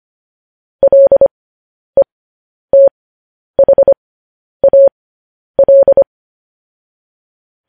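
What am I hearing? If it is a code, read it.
Morse code, heard as a keyed tone, about 25 wpm, spaced as for 10 wpm, 550 Hz. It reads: LETHAL
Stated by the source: Morse code news headlines sent as a reading transcription, preceded by vvv.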